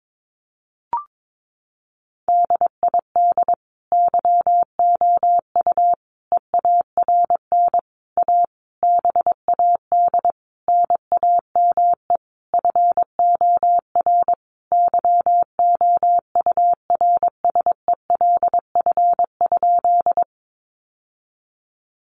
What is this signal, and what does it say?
Morse code, heard as a keyed tone, about 22 wpm, 700 Hz. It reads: DID YOU EARN A BAD NAME FOR YOURSELF?
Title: Sentences from Top 1500 Words 22wpm